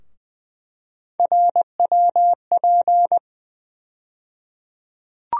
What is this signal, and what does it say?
Morse code, heard as a keyed tone, about 20 wpm, 700 Hz. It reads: RWP
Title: USA Call Sign Postfixes - 3 Letters 20wpm